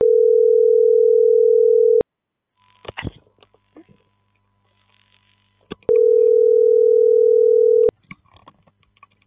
Ringback tone (North America), 440 + 480 Hz.